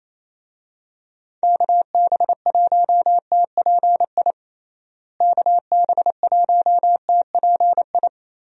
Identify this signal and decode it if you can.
Morse code, heard as a keyed tone, about 28 wpm, 700 Hz. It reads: KB1TPI KB1TPI
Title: USA Call Signs - 6 Characters - Repeated 3x - 28wpm